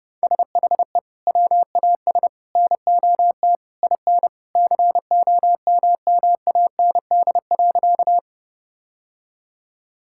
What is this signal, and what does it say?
Morse code, heard as a keyed tone, about 32 wpm, 700 Hz. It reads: SHE WAS NOT IN COMMAND.
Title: Sentences from Top 1600 Words 30wpm